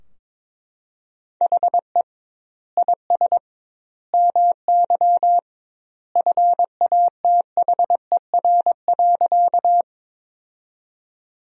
Morse code, heard as a keyed tone, about 22 wpm, 700 Hz. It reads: HE IS MY FATHER.
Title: 2x Word-Spacing - Sentences from Top 300 Words - 22wpm